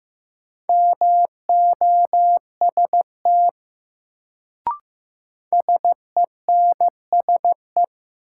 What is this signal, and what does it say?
Morse code, heard as a keyed tone, about 15 wpm, 700 Hz. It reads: MOST SENSE
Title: Sets of 2 Words - New Word Drill - 15wpm